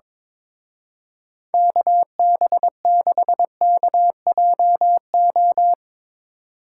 Morse code, heard as a keyed tone, about 24 wpm, 700 Hz. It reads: KB6KJO
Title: Speed Racing - USA 2x3 Call Signs 22wpm